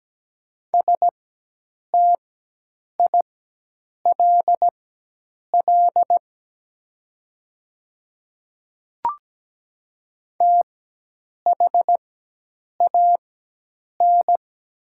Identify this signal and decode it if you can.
Morse code, heard as a keyed tone, about 17 wpm, spaced as for 8 wpm, 700 Hz. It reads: STILL THAN
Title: 4x Character Spacing - Top 300 Words - 17wpm